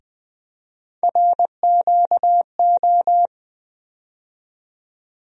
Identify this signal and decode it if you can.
Morse code, heard as a keyed tone, about 20 wpm, 700 Hz. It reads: RQO
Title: USA Call Sign Postfixes - 3 Letters 20wpm